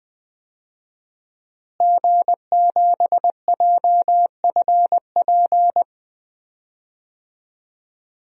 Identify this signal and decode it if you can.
Morse code, heard as a keyed tone, about 20 wpm, 700 Hz. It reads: G7JFP
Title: DXCC-100 Call Signs - 5 Characters - Repeated 2x - 20wpm